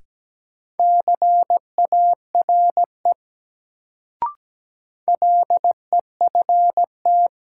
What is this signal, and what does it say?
Morse code, heard as a keyed tone, about 17 wpm, 700 Hz. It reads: CARE LEFT
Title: Sentences from Top 300 Words - New Word Drill - 17wpm